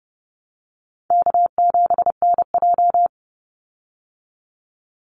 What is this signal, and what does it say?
Morse code, heard as a keyed tone, about 30 wpm, 700 Hz. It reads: K7NJ
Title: Call Signs Repeated 2x - 30wpm